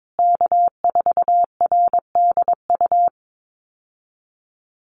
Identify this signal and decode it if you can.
Morse code, heard as a keyed tone, about 22 wpm, 690 Hz. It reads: K4RDU
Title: USA Call Signs Repeated 3x 22wpm